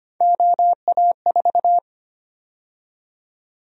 Morse code, heard as a keyed tone, about 25 wpm, 700 Hz. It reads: OA4
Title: USA Call Sign Prefixes - 25wpm